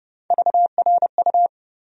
Morse code, heard as a keyed tone, about 30 wpm, 700 Hz. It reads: VRU
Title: USA Call Sign Postfixes - 3 Letters 30wpm